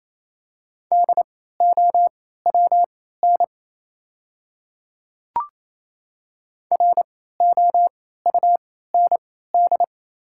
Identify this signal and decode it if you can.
Morse code, heard as a keyed tone, about 28 wpm, spaced as for 16 wpm, 700 Hz. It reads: DOWN ROUND
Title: 3x Character Spacing - Top 200 Words - 28wpm